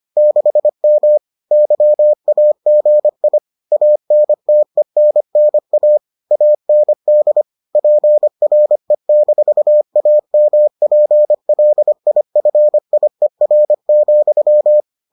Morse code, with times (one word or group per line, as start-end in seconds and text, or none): 0.2-1.2 6M
1.5-3.4 YAGI
3.7-6.0 ANTENNA
6.3-7.4 AND
7.8-14.8 PRE-AMPLIFIER,